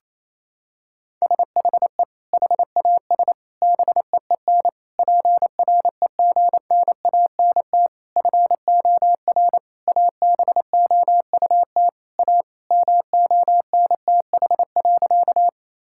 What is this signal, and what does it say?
Morse code, heard as a keyed tone, about 28 wpm, 700 Hz. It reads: SHE HAS BEEN PREGNANT FOR ABOUT A MONTH.
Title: Sentences from Top 1000 Words - Review Part 6 - 28wpm